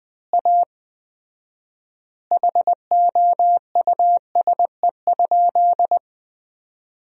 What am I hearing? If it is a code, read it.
Morse code, heard as a keyed tone, about 20 wpm, 700 Hz. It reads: A HOUSE?
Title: 4x Word-Spacing - Sentences from Top 200 Words - 20wpm